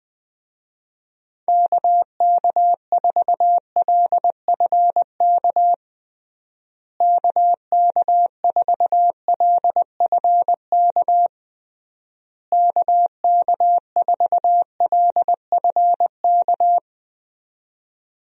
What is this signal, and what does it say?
Morse code, heard as a keyed tone, about 21 wpm, 700 Hz. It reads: KK4LFK KK4LFK KK4LFK
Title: USA Call Signs - 6 Characters - Repeated 3x - 20wpm